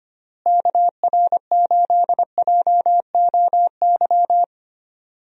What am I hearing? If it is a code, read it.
Morse code, heard as a keyed tone, about 25 wpm, 700 Hz. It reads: KR8JOY
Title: USA Call Signs - 6 Characters - Repeated 3x - 25wpm